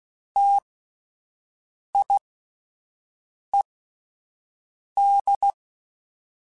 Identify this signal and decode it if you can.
Morse code, heard as a keyed tone, about 16 wpm, spaced as for 5 wpm, 800 Hz. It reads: TIED
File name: Alphabetical / Alphabetical mp3s / Ts @5